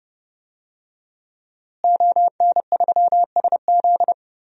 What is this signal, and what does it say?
Morse code, heard as a keyed tone, about 29 wpm, 700 Hz. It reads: ON3SZ
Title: DXCC-100 Call Signs - 5 Characters - Repeated 3x - 30wpm